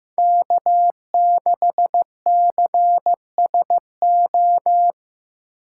Morse code, heard as a keyed tone, about 18 wpm, 700 Hz. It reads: K6CSO
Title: Speed Racing - USA 1x3 Call Signs 15wpm